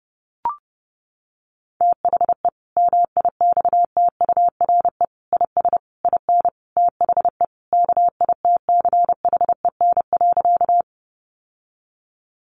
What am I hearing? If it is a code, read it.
Morse code, heard as a keyed tone, about 30 wpm, 700 Hz. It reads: THE MIXTURE IS IN THE KITCHEN.
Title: Sentences from Top 1500 Words 30wpm